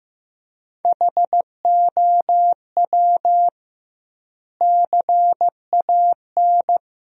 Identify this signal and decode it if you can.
Morse code, heard as a keyed tone, about 15 wpm, 700 Hz. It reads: HOW CAN
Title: 2x Word-Spacing - Sentences from Top 100 Words - 15wpm